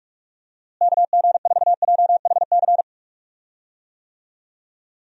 Morse code, heard as a keyed tone, about 44 wpm, 700 Hz. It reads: KG4JHC
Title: USA Call Signs - 6 Characters - Repeated 2x - 45wpm